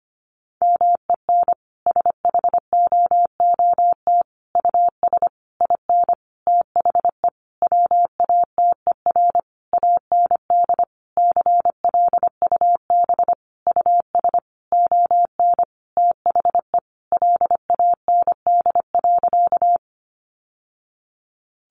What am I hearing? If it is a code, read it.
Morse code, heard as a keyed tone, about 25 wpm, 700 Hz. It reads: MEN SHOOT US IN THE WATER AND CLUB US ON THE LAND.
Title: Sentences from Top 1200 Words 25wpm